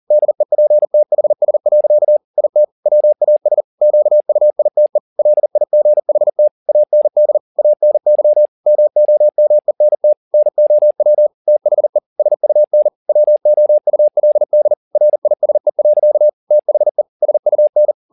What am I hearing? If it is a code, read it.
Morse code, heard as a keyed tone, about 40 wpm, 600 Hz. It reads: DEPTHS. IT WAS QUITE LIGHT AND ANY MOMENT NOW THE SUN WOULD RISE. THE SUN